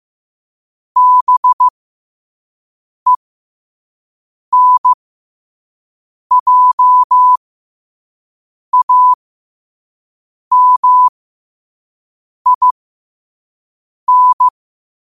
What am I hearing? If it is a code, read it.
Morse code, heard as a keyed tone, about 15 wpm, spaced as for 5 wpm, 1000 Hz. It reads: BENJAMIN
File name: qotdfiles / 5wpm